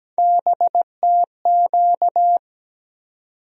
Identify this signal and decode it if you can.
Morse code, heard as a keyed tone, about 17 wpm, 700 Hz. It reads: BTQ
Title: USA Call Sign Postfixes - 3 Letters 17wpm